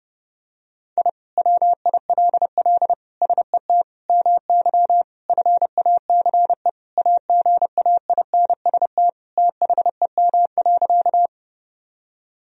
Morse code, heard as a keyed tone, about 30 wpm, 700 Hz. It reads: I WILL SET MY FACE AGAINST THEM.
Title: Sentences from Top 1000 Words - Review Part 2 - 30wpm